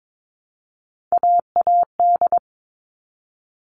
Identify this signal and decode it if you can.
Morse code, heard as a keyed tone, about 22 wpm, 700 Hz. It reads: AAD
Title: All Cut Numbers - 3-Digit 22wpm